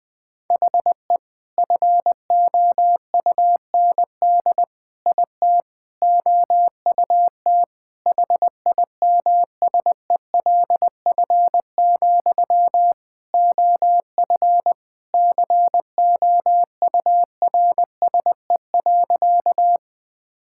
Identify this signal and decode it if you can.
Morse code, heard as a keyed tone, about 21 wpm, 700 Hz. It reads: HE FOUND IT OUT HIMSELF, OF COURSE.
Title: Sentences from Top 1600 Words 20wpm